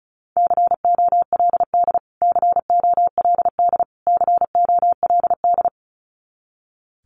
Morse code, heard as a keyed tone, about 34 wpm, 700 Hz. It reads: COLD COLD COLD
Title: Top 500 Words Repeated 35wpm